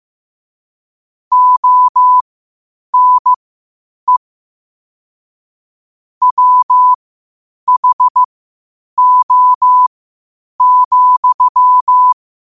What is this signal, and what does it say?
Morse code, heard as a keyed tone, about 15 wpm, spaced as for 8 wpm, 1000 Hz. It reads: ONE WHO,
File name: qotdfiles / 8wpm